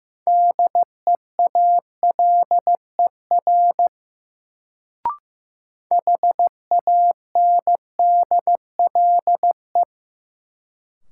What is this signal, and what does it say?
Morse code, heard as a keyed tone, about 16 wpm, 700 Hz. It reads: DEALER HANDLE
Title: Sentences from Top 1100 Words - New Word Drill - 15wpm